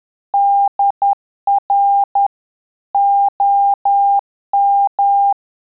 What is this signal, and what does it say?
Morse code, heard as a keyed tone, about 10 wpm, 800 Hz. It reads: DR OM
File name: Telegraphy-group / QSO CW 10 PPM